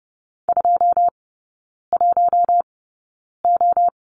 Morse code, heard as a keyed tone, about 28 wpm, spaced as for 9 wpm, 690 Hz. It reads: 21O
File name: Lesson 16e - Letter c - Three Characters 30wpm